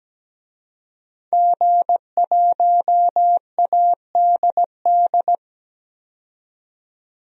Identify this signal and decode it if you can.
Morse code, heard as a keyed tone, about 17 wpm, 700 Hz. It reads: G1ADD
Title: DXCC-100 Call Signs - 5 Characters - 17wpm